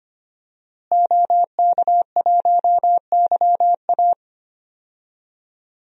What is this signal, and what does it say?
Morse code, heard as a keyed tone, about 25 wpm, 700 Hz. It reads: OK1YA
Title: DXCC-100 Call Signs - 5 Characters - Repeated 2x - 25wpm